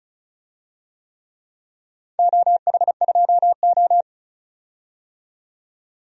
Morse code, heard as a keyed tone, about 36 wpm, 700 Hz. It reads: OH2O
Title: DXCC-100 Call Signs - 4 Characters - 35wpm